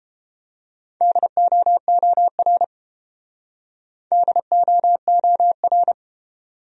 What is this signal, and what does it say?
Morse code, heard as a keyed tone, about 29 wpm, 700 Hz. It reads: DOOR DOOR